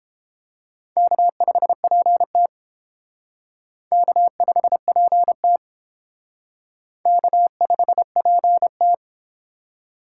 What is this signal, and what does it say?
Morse code, heard as a keyed tone, about 28 wpm, 700 Hz. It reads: K5PT K5PT K5PT